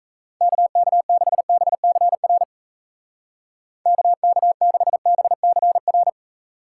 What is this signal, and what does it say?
Morse code, heard as a keyed tone, about 34 wpm, 700 Hz. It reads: KK6BCR KK6BCR
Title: Speed Racing - USA 2x3 Call Signs 28wpm